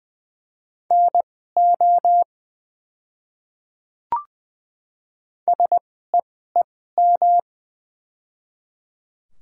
Morse code, heard as a keyed tone, about 20 wpm, spaced as for 14 wpm, 700 Hz. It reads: NO SEEM
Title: 2x Character Spacing - Top 300 Words - 20wpm